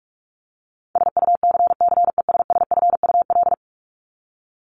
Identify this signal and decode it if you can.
Morse code, heard as a keyed tone, about 44 wpm, 700 Hz. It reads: SUCCESSFUL
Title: LY Suffix - Top 100 Words with Base Words 45wpm